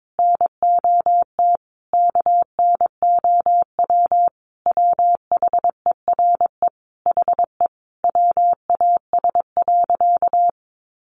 Morse code, heard as a keyed tone, about 22 wpm, 700 Hz. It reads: NOT KNOW WHERE HE WAS.
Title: Sentences from Top 600 Words 22wpm